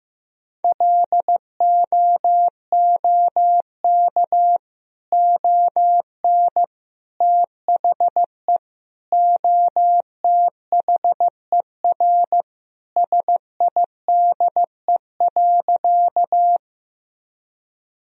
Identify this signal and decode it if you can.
Morse code, heard as a keyed tone, about 16 wpm, 700 Hz. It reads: LOOK ON THE OTHER SIDE.
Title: Sentences from Top 1000 Words - Review Part 1 - 15wpm